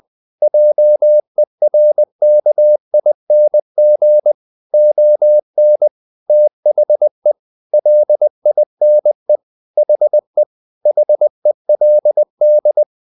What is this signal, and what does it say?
Morse code, heard as a keyed tone, about 20 wpm, 600 Hz. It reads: JERKING ON THE LINE HE HELD
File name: the old man and the sea part 2 20wpm